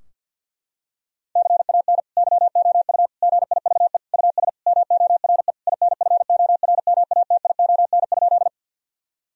Morse code, heard as a keyed tone, about 50 wpm, 700 Hz. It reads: CAN YOU GIVE US MORE INFORMATION?
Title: Sentences from Top 1000 Words - Review Part 3 - 50wpm